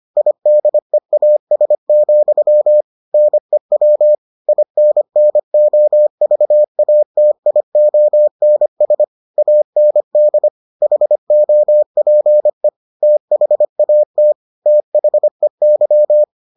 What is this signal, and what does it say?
Morse code, heard as a keyed tone, about 25 wpm, 600 Hz. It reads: IDEAS, NEW INNOVATIONS AND HOPE THAT THEY